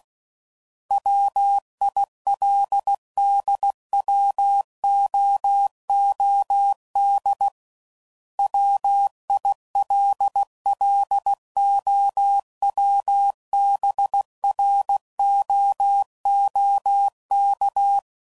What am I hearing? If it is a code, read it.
Morse code, heard as a keyed tone, about 16 wpm, 800 Hz. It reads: WILDWOOD WILLOWBROOK
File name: CW Code Groups @15WPM / fromSM's / Cities